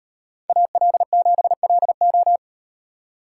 Morse code, heard as a keyed tone, about 38 wpm, 700 Hz. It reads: AL7LO